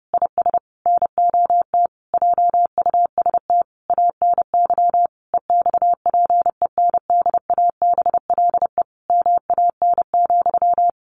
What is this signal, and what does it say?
Morse code, heard as a keyed tone, about 28 wpm, 700 Hz. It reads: IS NOT JUST ANY EXPENDABLE MAN,
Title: A Sound Of Thunder 30wpm